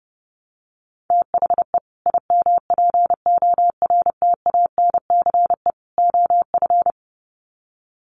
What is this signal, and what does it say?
Morse code, heard as a keyed tone, about 28 wpm, 700 Hz. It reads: THE IMPORTANCE OF
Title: Sets of 3 Words 30wpm